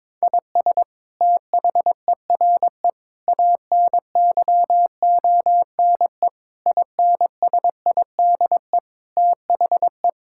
Morse code, heard as a keyed tone, about 22 wpm, 700 Hz. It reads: IS THERE ANYONE INSIDE THE
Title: Sentences from Top 1000 Words - Review Part 6 - 22wpm